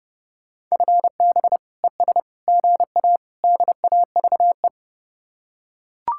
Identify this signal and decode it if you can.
Morse code, heard as a keyed tone, about 31 wpm, 700 Hz. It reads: FB ES GA DAVE